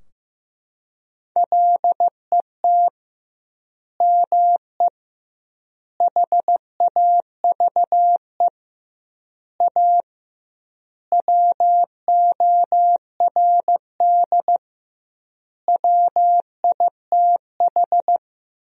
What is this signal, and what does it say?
Morse code, heard as a keyed tone, about 15 wpm, 700 Hz. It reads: LET ME HAVE A WORD WITH
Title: 2x Word-Spacing - Sentences from Top 300 Words - 15wpm